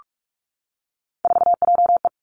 Morse code, heard as a keyed tone, about 46 wpm, 700 Hz. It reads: 4JE
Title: Three Challenging Letter-Number Combinations 45wpm